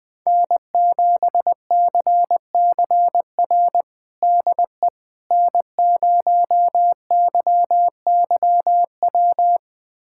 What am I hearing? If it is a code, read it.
Morse code, heard as a keyed tone, about 20 wpm, 700 Hz. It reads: N7CCR DE N0YYW
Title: USA Callsign DE Callsign - 5 Characters - 20wpm